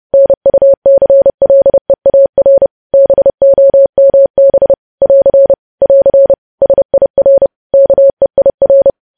Morse code, heard as a keyed tone, about 30 wpm, 550 Hz. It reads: NUCLEAR BOMB <AR> <AR> SIR KEIR
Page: News Headlines in Morse Code at 30 WPM